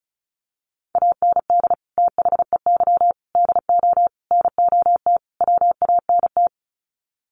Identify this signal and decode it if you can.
Morse code, heard as a keyed tone, about 34 wpm, 700 Hz. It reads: AND THEY DO NOT WANT
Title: Sets of 5 Words Encore2 35wpm